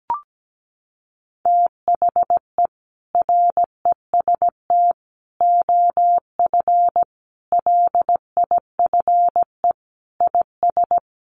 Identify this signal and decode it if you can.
Morse code, heard as a keyed tone, about 17 wpm, 700 Hz. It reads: THE REST OF LIFE IS
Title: Sentences from Top 1500 Words 17wpm